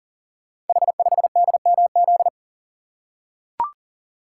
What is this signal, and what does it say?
Morse code, heard as a keyed tone, about 38 wpm, 700 Hz. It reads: H5DMZ
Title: Code Groups - 5 Letter:Numbers - 40wpm